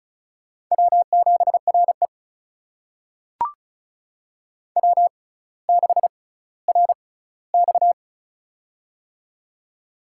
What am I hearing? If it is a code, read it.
Morse code, heard as a keyed tone, about 36 wpm, spaced as for 18 wpm, 700 Hz. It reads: W7RE W6RX
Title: USA Call Signs - 4 Characters - 6x Character Spacing - 35wpm